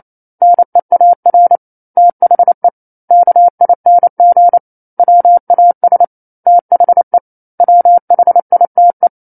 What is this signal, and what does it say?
Morse code, heard as a keyed tone, about 28 wpm, 700 Hz. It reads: NEAR THE KING WAS THE WHITE
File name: 09-Chapter-09 28 WPM